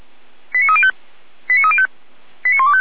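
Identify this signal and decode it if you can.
Five-tone selective call (ZVEI), heard as quick runs of five tones, tones of about 70 ms. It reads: 89305 89305 89125
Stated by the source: ZVEI five-tone call received on an SDR, from the feuerwehr-orlamuende.org fire brigade server.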